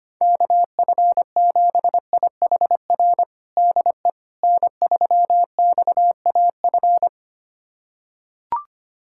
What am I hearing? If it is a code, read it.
Morse code, heard as a keyed tone, about 25 wpm, 700 Hz. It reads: KF7IHR DE N3XAF
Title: USA Callsign DE Callsign 25wpm